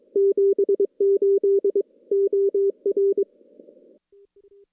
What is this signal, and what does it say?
Morse code, heard as a keyed tone, about 22 wpm, 400 Hz. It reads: 78 OR
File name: contests / CW-SS-k5tr- -2300